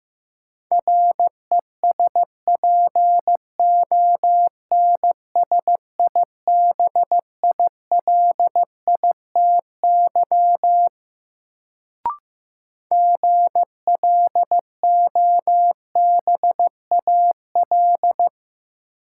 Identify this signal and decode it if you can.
Morse code, heard as a keyed tone, about 15 wpm, 700 Hz. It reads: RESPONSIBILITY GLOBAL
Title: Sentences from Top 1200 Words - New Word Drill Alt - 15wpm